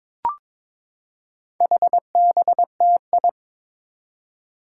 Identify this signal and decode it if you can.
Morse code, heard as a keyed tone, about 22 wpm, 700 Hz. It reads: HBTI
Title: Four Challenging Letter-Number Combinations 22wpm